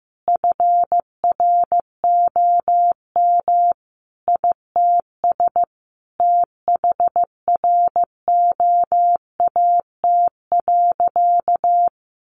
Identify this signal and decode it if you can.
Morse code, heard as a keyed tone, about 15 wpm, 700 Hz. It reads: FROM ITS THROAT.
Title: A Sound Of Thunder 15wpm